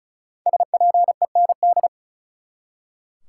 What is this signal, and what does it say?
Morse code, heard as a keyed tone, about 36 wpm, 700 Hz. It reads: SPEND
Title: Five Letter Words - 35wpm